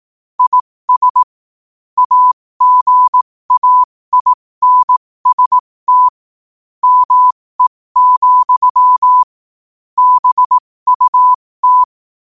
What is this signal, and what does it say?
Morse code, heard as a keyed tone, about 18 wpm, spaced as for 15 wpm, 1000 Hz. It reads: IS AGAINST ME, BUT
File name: qotdfiles / 15wpm